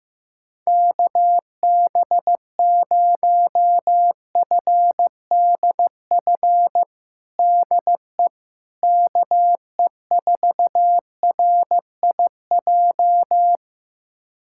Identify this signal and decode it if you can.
Morse code, heard as a keyed tone, about 15 wpm, 700 Hz. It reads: KB0FDF DE KE4RIJ
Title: USA Callsign DE Callsign 15wpm